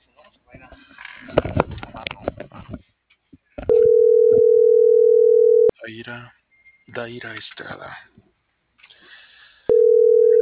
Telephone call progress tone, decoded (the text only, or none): ringback tone